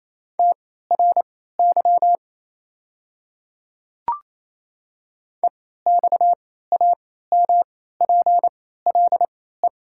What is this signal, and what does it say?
Morse code, heard as a keyed tone, about 28 wpm, spaced as for 16 wpm, 700 Hz. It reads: TRY EXAMPLE
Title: 3x Character Spacing - Top 300 Words - 28wpm